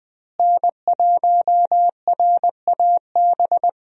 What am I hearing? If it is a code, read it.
Morse code, heard as a keyed tone, about 20 wpm, 700 Hz. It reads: N1RAB
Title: USA Call Signs - 5 Characters - Repeated 2x - 20wpm